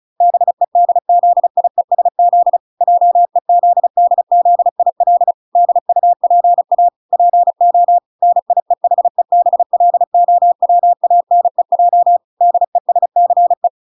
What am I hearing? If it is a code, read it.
Morse code, heard as a keyed tone, about 34 wpm, 700 Hz. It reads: BEDZIESZ JEZDZIL DUPA PO NIEHEBLOWANEJ DESCE